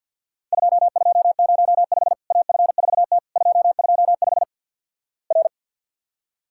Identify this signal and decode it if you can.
Morse code, heard as a keyed tone, about 50 wpm, 700 Hz. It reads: 2205 AF4T 225 R
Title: Worked All Europe DX Contest - QTC Exchange - 50wpm